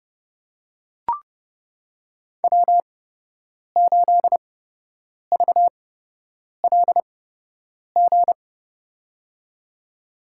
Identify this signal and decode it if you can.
Morse code, heard as a keyed tone, about 32 wpm, spaced as for 8 wpm, 700 Hz. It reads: W8VLG
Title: USA Call Signs - 5 Characters - 8x Character Spacing - 30wpm